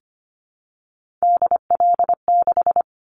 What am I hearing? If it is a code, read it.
Morse code, heard as a keyed tone, about 25 wpm, 700 Hz. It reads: DL6
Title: Callsign Prefixes for Top 25 DXCC - 25wpm